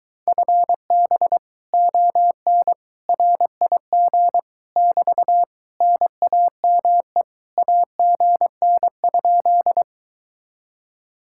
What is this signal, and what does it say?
Morse code, heard as a keyed tone, about 23 wpm, 700 Hz. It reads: FB ON RIG <BT> NAME AGN?